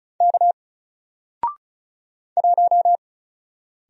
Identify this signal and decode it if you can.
Morse code, heard as a keyed tone, about 34 wpm, 700 Hz. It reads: K 1
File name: Single-Letter-Number-35wpm